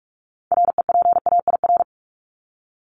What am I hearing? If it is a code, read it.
Morse code, heard as a keyed tone, about 44 wpm, 700 Hz. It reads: REPAIR
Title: Top 1501-1750 Words 45wpm